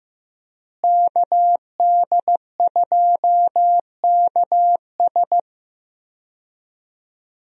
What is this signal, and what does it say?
Morse code, heard as a keyed tone, about 15 wpm, 700 Hz. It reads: KD2KS
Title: USA Call Signs - 5 Characters - Repeated 3x - 15wpm